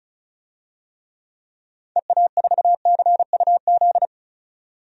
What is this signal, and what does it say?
Morse code, heard as a keyed tone, about 34 wpm, 700 Hz. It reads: EA4CUZ
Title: DXCC-100 Call Signs - 6 Characters - 35wpm